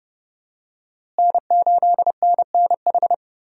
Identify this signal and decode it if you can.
Morse code, heard as a keyed tone, about 31 wpm, 700 Hz. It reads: N8NNH
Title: USA Call Signs - 5 Characters - Repeated 2x - 30wpm